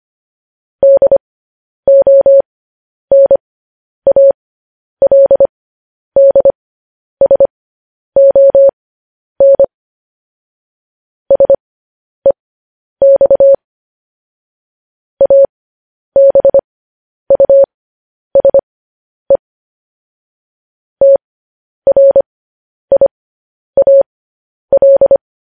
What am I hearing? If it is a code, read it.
Morse code, heard as a keyed tone, about 25 wpm, spaced as for 10 wpm, 550 Hz. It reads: DONALDSON SEX ABUSE TRIAL
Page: news-headlines-in-morse-code-at-10-wpm